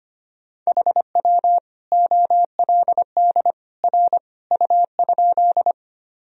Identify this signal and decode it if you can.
Morse code, heard as a keyed tone, about 25 wpm, 700 Hz. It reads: HW OLD R U?
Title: Sentences from Top 200 Words and Abbreviations 25wpm